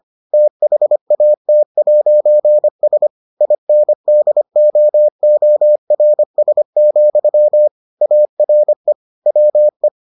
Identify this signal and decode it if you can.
Morse code, heard as a keyed tone, about 25 wpm, 600 Hz. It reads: THAT'S INDOORS, ARE WE